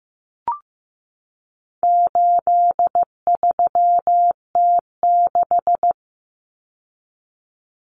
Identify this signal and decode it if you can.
Morse code, heard as a keyed tone, about 15 wpm, 700 Hz. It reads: 83T6
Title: Common Cut Numbers - 4-Digit 15wpm